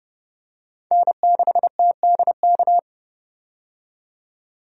Morse code, heard as a keyed tone, about 31 wpm, 700 Hz. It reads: N6TDK
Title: USA Call Signs - 5 Characters - 30wpm